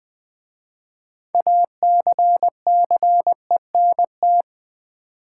Morse code, heard as a keyed tone, about 20 wpm, 700 Hz. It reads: ACCENT
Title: 6 Letter Words - First 3 Letters Given - 20wpm